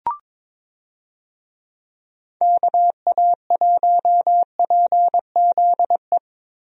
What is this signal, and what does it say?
Morse code, heard as a keyed tone, about 22 wpm, 700 Hz. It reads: KA1PZE
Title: USA Call Signs - 6 Characters - Repeated 3x - 22wpm